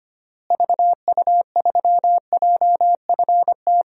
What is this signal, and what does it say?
Morse code, heard as a keyed tone, about 25 wpm, 700 Hz. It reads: VU3JFT
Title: DXCC-100 Call Signs - 6 Characters - Repeated 3x - 25wpm